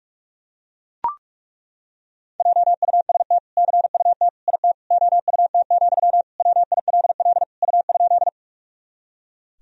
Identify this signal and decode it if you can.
Morse code, heard as a keyed tone, about 41 wpm, 700 Hz. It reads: JUST CUT IT OUT, WILL U?
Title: Sentences from Top 300 Words and Abbreviations 45wpm